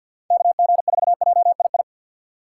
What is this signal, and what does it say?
Morse code, heard as a keyed tone, about 48 wpm, 700 Hz. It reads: KG4JII
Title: USA Call Signs - 6 Characters - 3x Character Spacing - 50wpm